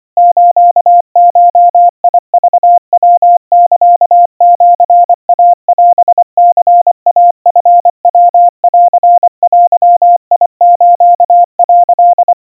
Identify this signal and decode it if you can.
Morse code (Japanese Wabun code), heard as a keyed tone, about 24 wpm, 700 Hz. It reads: スゴクヤサシイオニイチヤンデス」
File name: 90 cpm 700 Hz Crispy